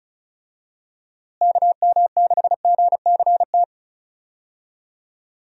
Morse code, heard as a keyed tone, about 36 wpm, 700 Hz. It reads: KM6GCT
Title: USA Call Signs - 6 Characters - Repeated 2x - 35wpm